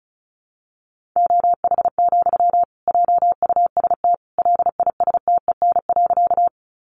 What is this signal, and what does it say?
Morse code, heard as a keyed tone, about 34 wpm, 700 Hz. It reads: OH, JUST LISTEN.
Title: Sentences from Top 500 Words 35wpm